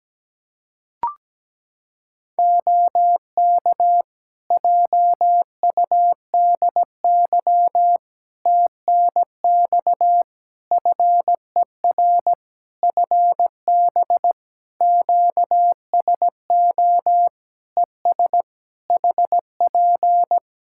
Morse code, heard as a keyed tone, about 17 wpm, 700 Hz. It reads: OK JUDY TNX FER FB QSO ES HP
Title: Practice QSO - Exchange 4 - Ending - 17wpm